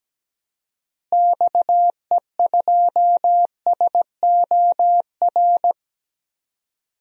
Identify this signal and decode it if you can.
Morse code, heard as a keyed tone, about 17 wpm, 700 Hz. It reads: XE2SOR
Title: DXCC-100 Call Signs - 6 Characters - Repeated 2x - 17wpm